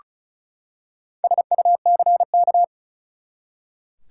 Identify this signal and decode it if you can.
Morse code, heard as a keyed tone, about 36 wpm, 700 Hz. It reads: SUCK
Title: Four Letter Words - 35wpm